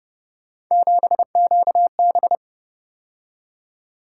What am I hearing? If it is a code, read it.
Morse code, heard as a keyed tone, about 29 wpm, 700 Hz. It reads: 7QB